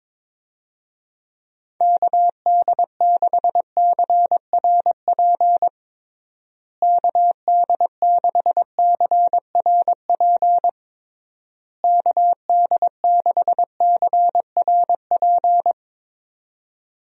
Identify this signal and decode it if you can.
Morse code, heard as a keyed tone, about 22 wpm, 700 Hz. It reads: KD6CRP KD6CRP KD6CRP
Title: USA Call Signs - 6 Characters - Repeated 3x - 22wpm